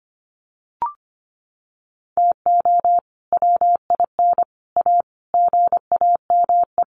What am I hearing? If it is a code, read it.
Morse code, heard as a keyed tone, about 25 wpm, 700 Hz. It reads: TO WIN A GAME
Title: Sentences from Top 1500 Words 25wpm